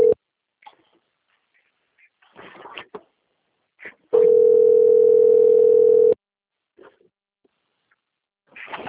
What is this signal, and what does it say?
Ringback tone (North America), 440 + 480 Hz.
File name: rg-1600-unknown- - - .7144